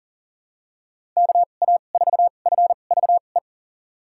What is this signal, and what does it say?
Morse code, heard as a keyed tone, about 40 wpm, spaced as for 29 wpm, 700 Hz. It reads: KA4FVE
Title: USA Call Signs - 6 Characters - 2x Character Spacing - 40wpm